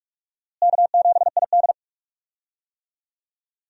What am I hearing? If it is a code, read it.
Morse code, heard as a keyed tone, about 44 wpm, 700 Hz. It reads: K7ID
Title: USA Call Signs - 4 Characters - Repeated 2x - 45wpm